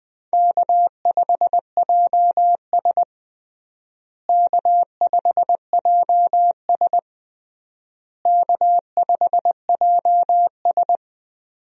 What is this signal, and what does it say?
Morse code, heard as a keyed tone, about 20 wpm, 700 Hz. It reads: K5JS K5JS K5JS
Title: USA Call Signs - 4 Characters - Repeated 3x - 20wpm